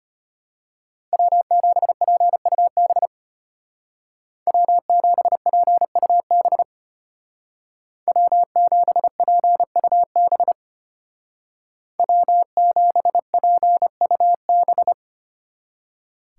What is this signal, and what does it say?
Morse code, heard as a keyed tone, about 30 wpm, 700 Hz. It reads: W7PUB W7PUB W7PUB W7PUB
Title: Speed Racing - USA 1x3 Call Signs 25wpm